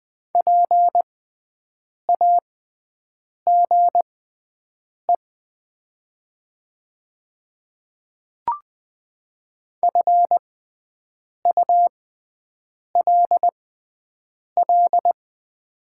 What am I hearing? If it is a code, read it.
Morse code, heard as a keyed tone, about 20 wpm, spaced as for 7 wpm, 700 Hz. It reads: PAGE FULL
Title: 6x Character Spacing - Top 500 Words - 20wpm